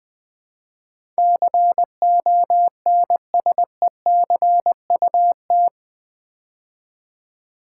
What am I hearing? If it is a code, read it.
Morse code, heard as a keyed tone, about 20 wpm, 700 Hz. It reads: CONSECUT
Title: Build A Word - 4 Letters At A Time - 11-Letter Words - 20wpm